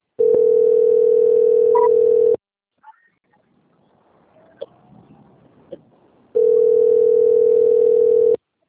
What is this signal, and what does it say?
Ringback tone (North America), 440 + 480 Hz.
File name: out- - - - - .51956